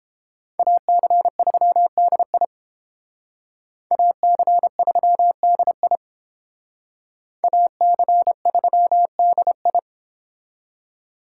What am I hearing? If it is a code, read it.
Morse code, heard as a keyed tone, about 30 wpm, 700 Hz. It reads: AC3DI AC3DI AC3DI